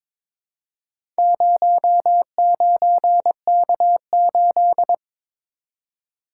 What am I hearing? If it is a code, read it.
Morse code, heard as a keyed tone, about 22 wpm, 700 Hz. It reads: 09K8
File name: practice / 22wpm / Four Letter-Number Combinations - 22wpm